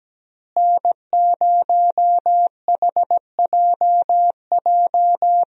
Morse code, heard as a keyed tone, about 17 wpm, 700 Hz. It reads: N0HJJ
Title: USA Call Signs - 5 Characters - Repeated 2x - 17wpm